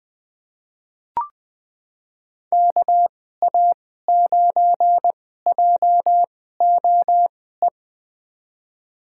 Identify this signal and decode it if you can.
Morse code, heard as a keyed tone, about 21 wpm, spaced as for 15 wpm, 700 Hz. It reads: KA9JOE
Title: USA Call Signs - 6 Characters - 2x Character Spacing - 20wpm